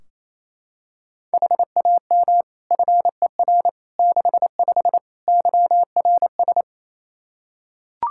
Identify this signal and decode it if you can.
Morse code, heard as a keyed tone, about 28 wpm, 700 Hz. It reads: HAM FER 65 YRS
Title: Practice QSO - Exchange 3 - More Info - 28wpm